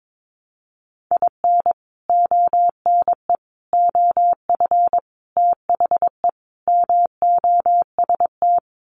Morse code, heard as a keyed tone, about 22 wpm, 700 Hz. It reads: IN ONE OF THE MOST
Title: Sets of 5 Words Encore 22wpm